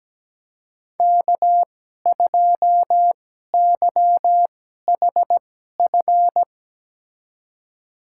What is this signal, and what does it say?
Morse code, heard as a keyed tone, about 17 wpm, spaced as for 12 wpm, 700 Hz. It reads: K2YHF
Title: USA Call Signs - 5 Characters - 2x Character Spacing - 17wpm